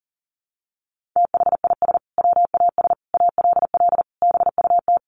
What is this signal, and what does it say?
Morse code, heard as a keyed tone, about 40 wpm, 700 Hz. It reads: THIS WAS ALL BUT
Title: Sentences from Top 1400 Words 40wpm